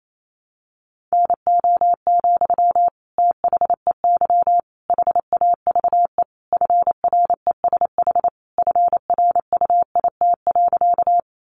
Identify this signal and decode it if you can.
Morse code, heard as a keyed tone, about 28 wpm, 700 Hz. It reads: NO, THEY HAVE FRESH FRUIT.
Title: Sentences from Top 1300 Words 28wpm